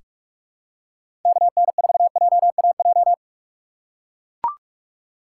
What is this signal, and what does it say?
Morse code, heard as a keyed tone, about 44 wpm, 700 Hz. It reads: KN4JAJ
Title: USA Call Signs - 6x Character Spacing - 45wpm